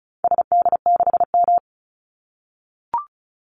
Morse code, heard as a keyed tone, about 36 wpm, 700 Hz. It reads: SD6M
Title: Call Signs - 35wpm